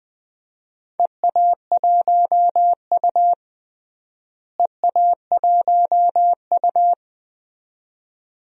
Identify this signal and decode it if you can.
Morse code, heard as a keyed tone, about 20 wpm, 700 Hz. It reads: EA1U EA1U
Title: DXCC-100 Call Signs - 4 Characters - Repeated 2x - 20wpm